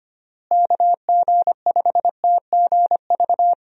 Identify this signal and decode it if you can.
Morse code, heard as a keyed tone, about 25 wpm, 700 Hz. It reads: KG5TGV
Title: USA Call Signs - 6 Characters - Repeated 3x - 25wpm